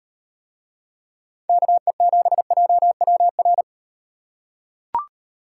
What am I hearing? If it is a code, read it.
Morse code, heard as a keyed tone, about 36 wpm, 700 Hz. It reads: KE7JWR